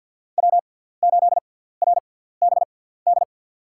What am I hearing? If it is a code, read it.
Morse code, heard as a keyed tone, about 52 wpm, spaced as for 17 wpm, 700 Hz. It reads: W8RBD
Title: USA Call Signs - 5 Characters - 6x Character Spacing - 50wpm